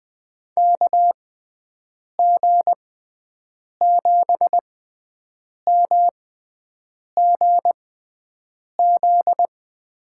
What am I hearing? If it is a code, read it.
Morse code, heard as a keyed tone, about 20 wpm, spaced as for 7 wpm, 700 Hz. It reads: KG7MGZ
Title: USA Call Signs - 6x Character Spacing - 20wpm